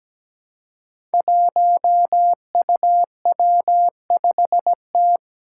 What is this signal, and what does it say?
Morse code, heard as a keyed tone, about 17 wpm, 700 Hz. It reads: 1UW5T
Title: Five Challenging Letter-Number Combinations 17wpm